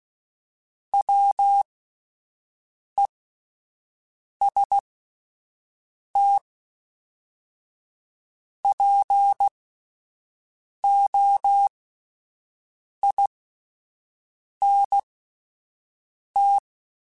Morse code, heard as a keyed tone, about 16 wpm, spaced as for 5 wpm, 800 Hz. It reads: WEST POINT